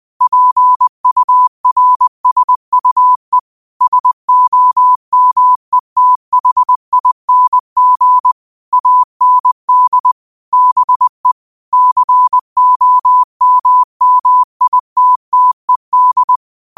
Morse code, heard as a keyed tone, about 20 wpm, 1000 Hz. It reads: PURSUE SOMETHING AND BE COMMITTED
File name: qotdfiles / 20wpm